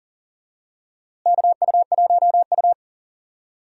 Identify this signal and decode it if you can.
Morse code, heard as a keyed tone, about 40 wpm, 700 Hz. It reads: KU1U
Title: USA Call Signs - 4 Characters - Repeated 2x - 40wpm